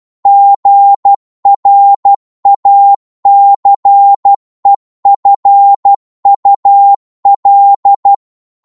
Morse code, heard as a keyed tone, about 12 wpm, 800 Hz. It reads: GRACEFUL